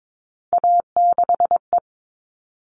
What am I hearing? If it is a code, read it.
Morse code, heard as a keyed tone, about 22 wpm, 700 Hz. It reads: A6E
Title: All Cut Numbers - 3-Digit 22wpm